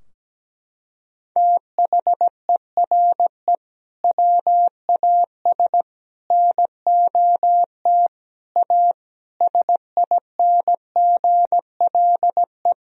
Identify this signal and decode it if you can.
Morse code, heard as a keyed tone, about 17 wpm, 700 Hz. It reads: THERE WAS NOT A SINGLE
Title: Sentences from Top 1000 Words - Review Part 4 - 17wpm